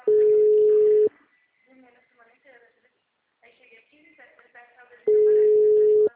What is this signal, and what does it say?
Ringback tone (Europe, 425 Hz), with speech.
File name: out- -2013- - - .7